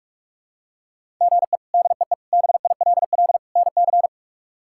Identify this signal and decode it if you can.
Morse code, heard as a keyed tone, about 46 wpm, 700 Hz. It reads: GE DEE BILL NC
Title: K1USN SST QSO Practice - 45wpm